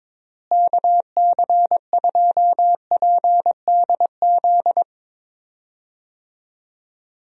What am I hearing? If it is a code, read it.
Morse code, heard as a keyed tone, about 22 wpm, 700 Hz. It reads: KC2PDZ
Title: USA Call Signs - 6 Characters - Repeated 3x - 22wpm